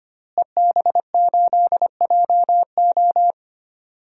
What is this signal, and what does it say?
Morse code, heard as a keyed tone, about 25 wpm, 700 Hz. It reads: EB8JO
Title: DXCC-100 Call Signs - 5 Characters - Repeated 2x - 25wpm